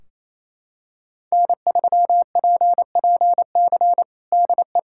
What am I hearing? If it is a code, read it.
Morse code, heard as a keyed tone, about 28 wpm, 700 Hz. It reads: N3PPC DE
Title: Practice QSO - Exchange 4 - Ending - 28wpm